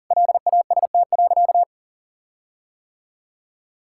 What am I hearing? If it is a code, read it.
Morse code, heard as a keyed tone, about 40 wpm, 700 Hz. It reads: LAST.
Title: Sentences from Top 1600 Words 40wpm